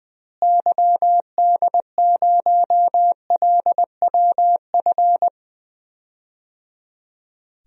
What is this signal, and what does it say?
Morse code, heard as a keyed tone, about 20 wpm, 700 Hz. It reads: YD0LWF
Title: DXCC-100 Call Signs - 6 Characters - Repeated 3x - 20wpm